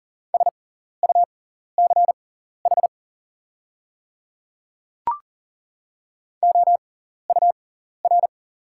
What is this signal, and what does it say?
Morse code, heard as a keyed tone, about 40 wpm, spaced as for 14 wpm, 690 Hz. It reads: SUCH OUR
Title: 6x Character Spacing - Top 300 Words - 40wpm